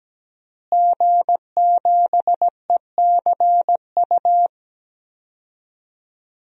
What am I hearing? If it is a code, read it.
Morse code, heard as a keyed tone, about 17 wpm, 700 Hz. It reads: G7ECU
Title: DXCC-100 Call Signs - 5 Characters - Repeated 3x - 17wpm